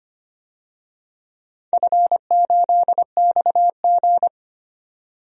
Morse code, heard as a keyed tone, about 25 wpm, 700 Hz. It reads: F8XG
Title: DXCC-100 Call Signs - 4 Characters - 25wpm